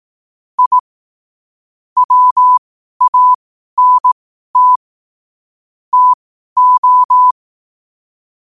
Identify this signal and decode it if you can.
Morse code, heard as a keyed tone, about 18 wpm, spaced as for 12 wpm, 1000 Hz. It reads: I WANT TO
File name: 12wpm